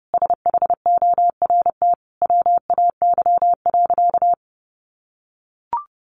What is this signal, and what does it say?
Morse code, heard as a keyed tone, about 30 wpm, 700 Hz. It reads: SHORT WAY.
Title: Sentences from Top 1400 Words 30wpm